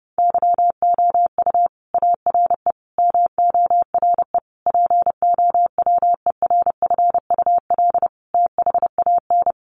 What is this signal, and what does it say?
Morse code, heard as a keyed tone, about 30 wpm, 700 Hz. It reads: YOU ARE MORE POWERFUL THAN
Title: Sentences from Top 1300 Words 30wpm